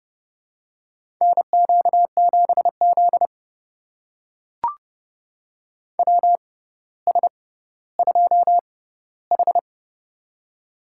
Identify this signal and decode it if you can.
Morse code, heard as a keyed tone, about 30 wpm, spaced as for 15 wpm, 700 Hz. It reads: NQ7Z WS2H
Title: USA Call Signs - 6x Character Spacing - 30wpm